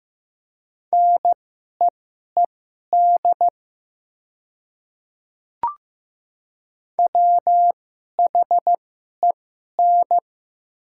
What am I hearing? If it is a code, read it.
Morse code, heard as a keyed tone, about 15 wpm, spaced as for 11 wpm, 690 Hz. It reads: NEED WHEN
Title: 2x Character Spacing - Top 300 Words - 15wpm